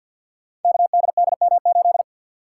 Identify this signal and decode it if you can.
Morse code, heard as a keyed tone, about 50 wpm, 700 Hz. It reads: KDDM8
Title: Code Groups - 5 Letter:Numbers - 50wpm